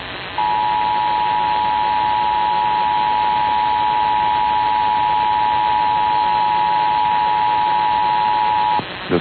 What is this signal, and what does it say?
Emergency Alert System (SAME) tones (the 853 and 960 Hz attention tone), with noise. It is heard from 0.5 to 8.8 s.
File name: eas / KUSO / KUSO